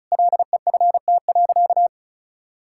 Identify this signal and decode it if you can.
Morse code, heard as a keyed tone, about 36 wpm, 700 Hz. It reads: LEFT.
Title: Sentences from Top 1000 Words - Review Part 6 - 35wpm